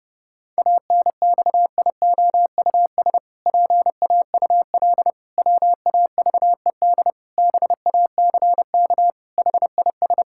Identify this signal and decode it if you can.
Morse code, heard as a keyed tone, about 32 wpm, 700 Hz. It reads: ANXIOUS PAUL WAVED BACK HIS
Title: Pangram Challenge v2 - 30wpm